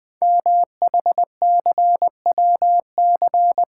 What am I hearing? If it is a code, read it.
Morse code, heard as a keyed tone, about 20 wpm, 700 Hz. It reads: MHCWC
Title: Code Groups - 5 Letter:Numbers - 20wpm